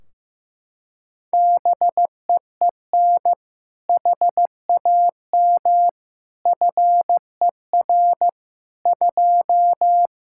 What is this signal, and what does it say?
Morse code, heard as a keyed tone, about 15 wpm, 700 Hz. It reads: BEEN HAM FER 2
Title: Practice QSO - Exchange 3 - More Info - 15wpm